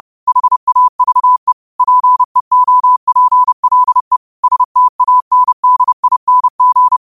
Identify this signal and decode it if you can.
Morse code, heard as a keyed tone, about 30 wpm, 1000 Hz. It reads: HAVE PEOPLE STANDING